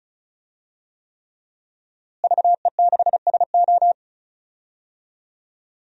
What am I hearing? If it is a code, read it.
Morse code, heard as a keyed tone, about 36 wpm, 700 Hz. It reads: VE6SO
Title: DXCC-100 Call Signs - 5 Characters - Repeated 3x - 35wpm